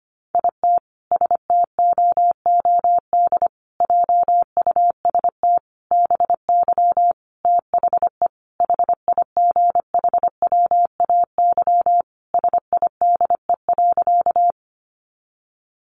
Morse code, heard as a keyed tone, about 25 wpm, 700 Hz. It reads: IT STOOD JUST BY THE HIGHWAY SIDE.
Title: Sentences from Top 1400 Words 25wpm